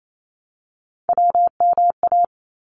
Morse code, heard as a keyed tone, about 28 wpm, 700 Hz. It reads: WMA
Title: ARRL Sections 28wpm